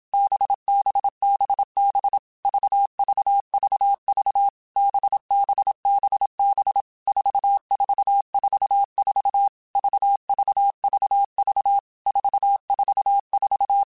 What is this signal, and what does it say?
Morse code, heard as a keyed tone, about 27 wpm, 800 Hz. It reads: BBBB VVVV BBBB 4444 VVVV 444
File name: Basic Letter Groups By7Groups / VB46 / VB46@25